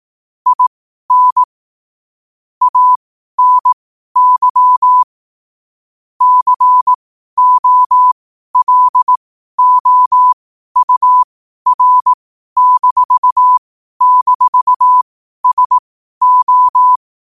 Morse code, heard as a keyed tone, about 18 wpm, spaced as for 12 wpm, 1000 Hz. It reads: IN ANY COLOUR--SO
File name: qotdfiles / 12wpm